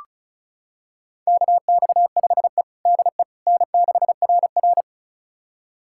Morse code, heard as a keyed tone, about 36 wpm, 700 Hz. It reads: KX5E DE N6RR
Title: USA Callsign DE Callsign - 4 Characters - 35wpm